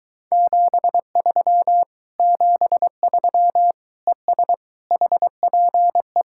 Morse code, heard as a keyed tone, about 23 wpm, 700 Hz. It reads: 73 73 ES HPE